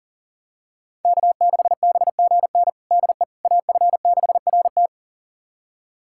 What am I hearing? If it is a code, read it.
Morse code, heard as a keyed tone, about 37 wpm, 700 Hz. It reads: K6BGN DE AF6RT
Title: USA Callsign DE Callsign - 5 Characters - 40wpm